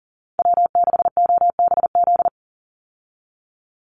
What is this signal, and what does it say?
Morse code, heard as a keyed tone, about 40 wpm, 700 Hz. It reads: W6OBZ
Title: USA Call Signs 40wpm